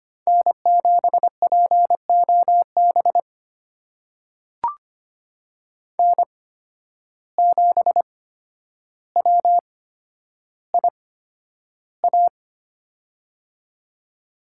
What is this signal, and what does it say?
Morse code, heard as a keyed tone, about 25 wpm, spaced as for 11 wpm, 700 Hz. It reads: N7POB N7WIA
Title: USA Call Signs - 5 Characters - 8x Character Spacing - 25wpm